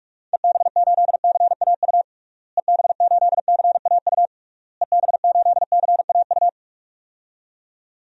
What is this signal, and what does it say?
Morse code, heard as a keyed tone, about 46 wpm, 700 Hz. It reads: EB8CAU EB8CAU EB8CAU
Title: DXCC-100 Call Signs - 6 Characters - Repeated 3x - 45wpm